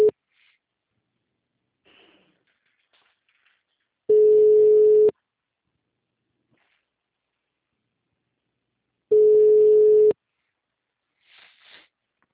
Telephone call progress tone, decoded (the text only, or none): ringback tone